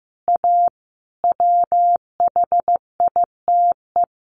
Morse code, heard as a keyed tone, about 15 wpm, 700 Hz. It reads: A WHITE
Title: Additional Easy Phrases 15wpm